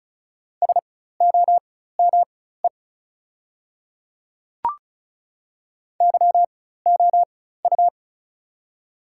Morse code, heard as a keyed tone, about 36 wpm, spaced as for 16 wpm, 700 Hz. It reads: SOME YOU
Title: 4x Character Spacing - Top 200 Words - 35wpm